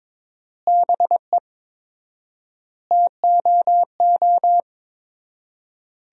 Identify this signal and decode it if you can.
Morse code, heard as a keyed tone, about 22 wpm, 700 Hz. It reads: BE TOO
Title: 4x Word-Spacing - Sentences from Top 300 Words - 22wpm